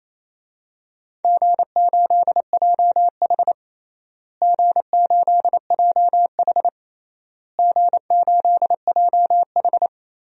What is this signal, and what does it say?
Morse code, heard as a keyed tone, about 28 wpm, 700 Hz. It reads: G8JH G8JH G8JH
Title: DXCC-100 Call Signs - 4 Characters - Repeated 3x - 28wpm